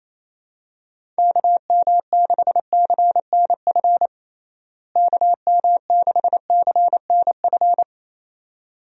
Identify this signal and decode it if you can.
Morse code, heard as a keyed tone, about 28 wpm, 700 Hz. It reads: KM6CNF KM6CNF
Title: USA Call Signs - 6 Characters - Repeated 2x - 28wpm